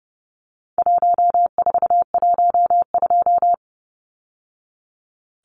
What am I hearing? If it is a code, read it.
Morse code, heard as a keyed tone, about 30 wpm, 700 Hz. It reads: 1412
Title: Numbers - 4-Digit 30wpm